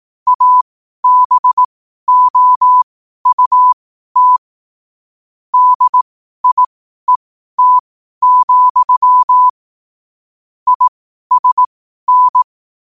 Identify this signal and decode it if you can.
Morse code, heard as a keyed tone, about 18 wpm, spaced as for 12 wpm, 1000 Hz. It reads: ABOUT DIET, ISN
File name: qotdfiles / 12wpm